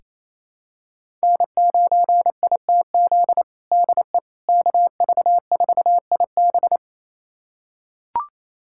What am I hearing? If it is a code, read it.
Morse code, heard as a keyed tone, about 27 wpm, 700 Hz. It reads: N9ITZ DE KV4IB
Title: USA Callsign DE Callsign - 5 Characters - 28wpm